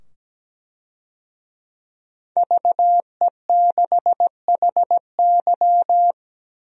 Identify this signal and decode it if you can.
Morse code, heard as a keyed tone, about 17 wpm, 700 Hz. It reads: VE6HY
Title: DXCC-100 Call Signs - 5 Characters - 17wpm